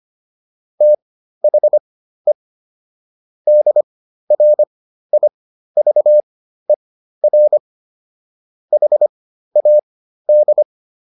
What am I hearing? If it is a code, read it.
Morse code, heard as a keyed tone, about 25 wpm, spaced as for 13 wpm, 600 Hz. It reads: THE DRIVER HAD